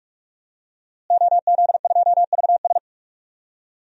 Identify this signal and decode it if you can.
Morse code, heard as a keyed tone, about 46 wpm, 700 Hz. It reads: OZ2VS
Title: DXCC-100 Call Signs - 5 Characters - Repeated 2x - 45wpm